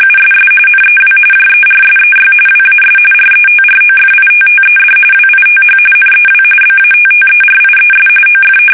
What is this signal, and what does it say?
RTTY, about 45.45 baud. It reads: USS NEW JERSEY WAS RETROFITTED WITH NEW BATTERIES OF
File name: fsk850 / battleshipnj